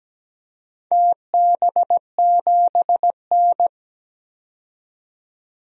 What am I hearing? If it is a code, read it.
Morse code, heard as a keyed tone, about 17 wpm, 700 Hz. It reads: TB7N
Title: DXCC-100 Call Signs - 4 Characters - 17wpm